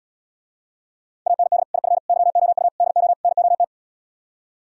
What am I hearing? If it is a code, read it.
Morse code, heard as a keyed tone, about 77 wpm, 700 Hz. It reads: AND IF YOU DO NOT